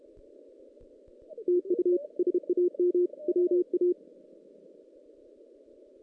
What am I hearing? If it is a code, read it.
Morse code, heard as a keyed tone, about 32 wpm, 360 Hz. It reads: TU SAM WA
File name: 2016 JAN CW NAQP / 2016 Jan naqp k5ot - 0-5